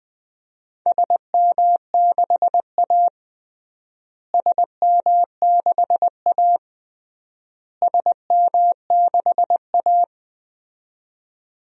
Morse code, heard as a keyed tone, about 21 wpm, 700 Hz. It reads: SM6A SM6A SM6A
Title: DXCC-100 Call Signs - 4 Characters - Repeated 3x - 20wpm